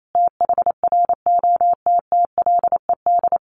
Morse code, heard as a keyed tone, about 28 wpm, 700 Hz. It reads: THROTTLED
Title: A Sound Of Thunder 28wpm